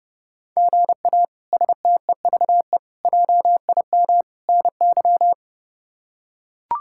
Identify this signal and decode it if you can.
Morse code, heard as a keyed tone, about 32 wpm, 700 Hz. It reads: GA STEVE JIM NY
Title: K1USN SST QSO Practice - 30wpm